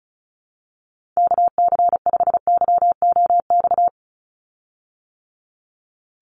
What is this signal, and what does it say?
Morse code, heard as a keyed tone, about 34 wpm, 700 Hz. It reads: KC5YOX
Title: USA Call Signs 35wpm